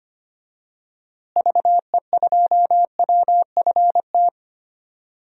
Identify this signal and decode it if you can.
Morse code, heard as a keyed tone, about 25 wpm, 700 Hz. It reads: VE2WFT
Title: DXCC-100 Call Signs - 6 Characters - Repeated 3x - 25wpm